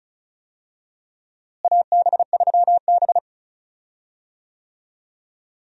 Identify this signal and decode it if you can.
Morse code, heard as a keyed tone, about 34 wpm, 700 Hz. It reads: AB3B
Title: USA Call Signs - 4 Characters - Repeated 3x - 35wpm